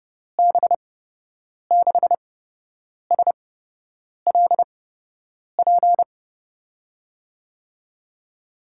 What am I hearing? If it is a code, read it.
Morse code, heard as a keyed tone, about 32 wpm, spaced as for 8 wpm, 700 Hz. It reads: B6SLP
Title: USA Call Signs - 6 Characters - 8x Character Spacing - 30wpm